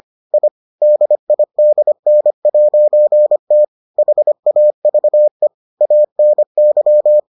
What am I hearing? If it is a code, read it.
Morse code, heard as a keyed tone, about 25 wpm, 600 Hz. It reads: I DIDN'T HAVE ANY